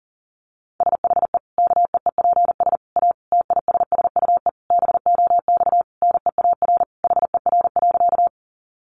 Morse code, heard as a keyed tone, about 40 wpm, 700 Hz. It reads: SHE KEEPS A TISSUE BOX NEAR HER.
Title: Easy Phrases 40wpm